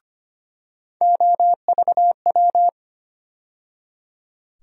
Morse code, heard as a keyed tone, about 25 wpm, 700 Hz. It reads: OVW
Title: USA Call Sign Postfixes - 3 Letters 25wpm